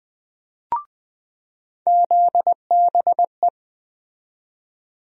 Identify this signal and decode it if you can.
Morse code, heard as a keyed tone, about 20 wpm, 700 Hz. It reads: ZBE
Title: USA Call Sign Postfixes - 3 Letters 20wpm